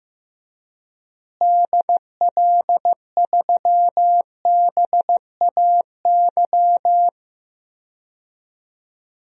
Morse code, heard as a keyed tone, about 15 wpm, 700 Hz. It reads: DL3BAY
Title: DXCC-100 Call Signs - 6 Characters - Repeated 3x - 15wpm